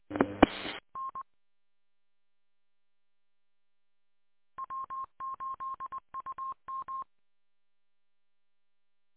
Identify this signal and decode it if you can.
Morse code, heard as a keyed tone, about 20 wpm, 1060 Hz. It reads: E E N W8UM